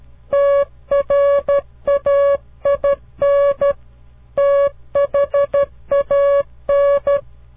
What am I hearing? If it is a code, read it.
Morse code, heard as a keyed tone, about 13 wpm, 560 Hz. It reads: TRAIN THAN
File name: CW Learning Programs / farnsworth / unit 2